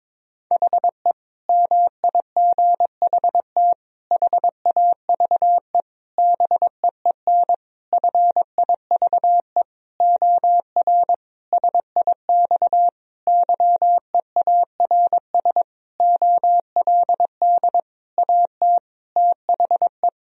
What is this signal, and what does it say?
Morse code, heard as a keyed tone, about 22 wpm, 700 Hz. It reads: HE MIGHT HAVE BEEN FIVE OR SIX YEARS OLD AT THE
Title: Sentences from Top 1000 Words - Review Part 2 - 22wpm